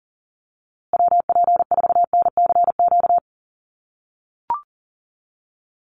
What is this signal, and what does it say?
Morse code, heard as a keyed tone, about 40 wpm, 700 Hz. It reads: WP4NCQ
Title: USA Call Signs Repeated 2x 40wpm